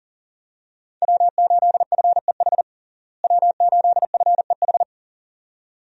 Morse code, heard as a keyed tone, about 40 wpm, 700 Hz. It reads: W8FEH W8FEH
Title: USA Call Signs - 5 Characters - Repeated 2x - 40wpm